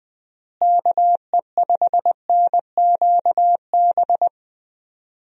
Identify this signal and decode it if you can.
Morse code, heard as a keyed tone, about 20 wpm, 700 Hz. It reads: KE5NQB
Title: USA Call Signs - 6 Characters - Repeated 2x - 20wpm